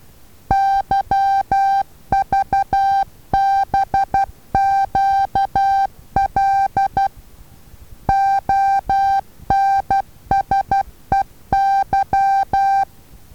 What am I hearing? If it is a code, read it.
Morse code, heard as a keyed tone, about 12 wpm, 780 Hz. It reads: YVBQL ONSEY
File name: Taviro HA6VA B